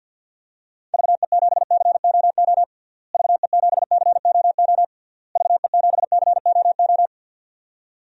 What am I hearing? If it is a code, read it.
Morse code, heard as a keyed tone, about 52 wpm, 700 Hz. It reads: VE7COO VE7COO VE7COO
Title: DXCC-100 Call Signs - 6 Characters - Repeated 3x - 50wpm